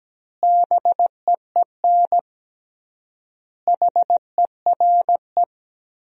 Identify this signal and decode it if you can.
Morse code, heard as a keyed tone, about 17 wpm, 700 Hz. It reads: BEEN HERE
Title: 3x Word-Spacing - Sentences from Top 200 Words - 17wpm